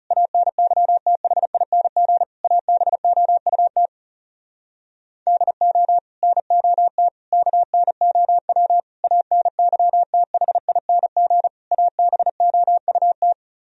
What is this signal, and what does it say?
Morse code, heard as a keyed tone, about 40 wpm, 700 Hz. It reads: ANYTHING ABOUT DO NOT KNOW ANYTHING ABOUT